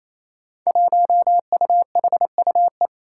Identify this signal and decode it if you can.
Morse code, heard as a keyed tone, about 28 wpm, 700 Hz. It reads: 1UHUE
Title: Five Challenging Letter-Number Combinations 28wpm